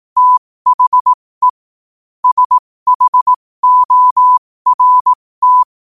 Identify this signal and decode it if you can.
Morse code, heard as a keyed tone, about 18 wpm, spaced as for 15 wpm, 1000 Hz. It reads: THE SHORT
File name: qotdfiles / 15wpm